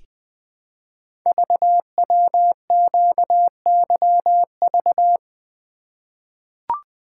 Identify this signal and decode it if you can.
Morse code, heard as a keyed tone, about 20 wpm, 700 Hz. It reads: VWQYV
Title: Code Groups - 5 Letters - 20wpm